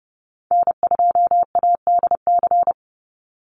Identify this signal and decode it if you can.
Morse code, heard as a keyed tone, about 30 wpm, 700 Hz. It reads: N2ADC
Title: Call Signs Repeated 3x - 30wpm